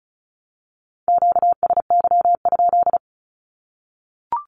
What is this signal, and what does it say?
Morse code, heard as a keyed tone, about 34 wpm, 700 Hz. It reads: QSY?
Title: Q-Codes 35wpm